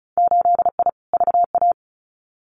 Morse code, heard as a keyed tone, about 34 wpm, 700 Hz. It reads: 8I VA
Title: Winter Field Day Exchanges Encore 35wpm